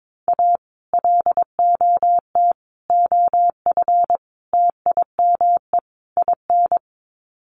Morse code, heard as a keyed tone, about 22 wpm, 700 Hz. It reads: A LOT OF TIME IN
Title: Sets of 5 Words Encore2 22wpm